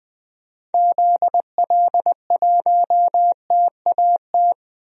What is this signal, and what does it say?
Morse code, heard as a keyed tone, about 21 wpm, 700 Hz. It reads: ZL1TAT
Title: DXCC-100 Call Signs - 6 Characters - Repeated 3x - 20wpm